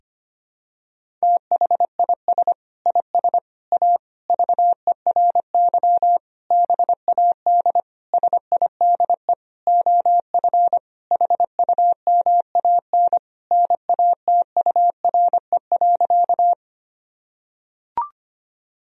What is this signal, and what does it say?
Morse code, heard as a keyed tone, about 25 wpm, 700 Hz. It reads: THIS IS A VERY BAD SIDE OF HUMAN NATURE.
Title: Sentences from Top 1000 Words - Review Part 3 - 25wpm